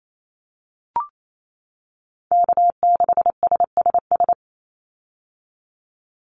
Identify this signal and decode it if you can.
Morse code, heard as a keyed tone, about 28 wpm, 700 Hz. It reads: K6SSS
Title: Call Signs - 28wpm